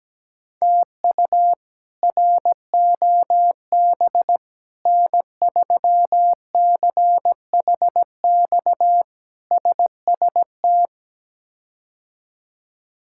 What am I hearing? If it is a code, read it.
Morse code, heard as a keyed tone, about 17 wpm, 700 Hz. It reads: TU ROB N3CHX SST
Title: K1USN SST QSO Practice - 17wpm